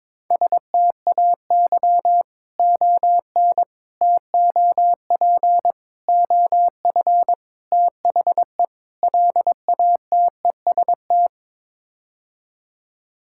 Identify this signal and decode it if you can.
Morse code, heard as a keyed tone, about 22 wpm, 700 Hz. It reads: STAY ON TOP OF THE LATEST
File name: practice / 22wpm / Sets of 6 Words - 22wpm